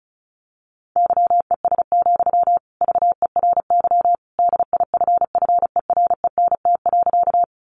Morse code, heard as a keyed tone, about 34 wpm, 700 Hz. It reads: YES, VERY DIFFERENT.
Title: Sentences from Top 600 Words 35wpm